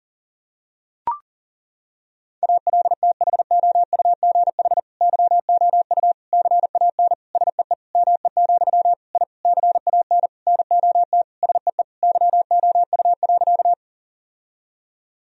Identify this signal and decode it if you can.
Morse code, heard as a keyed tone, about 40 wpm, 700 Hz. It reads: ALTHOUGH YOU CAN SEE ME, I CAN NOT SEE YOU.
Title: Sentences from Top 1000 Words - Review Part 3 - 40wpm